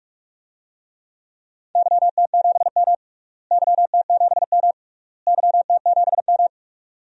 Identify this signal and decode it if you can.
Morse code, heard as a keyed tone, about 42 wpm, 700 Hz. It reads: YT7M YT7M YT7M
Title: DXCC-100 Call Signs - 4 Characters - Repeated 3x - 45wpm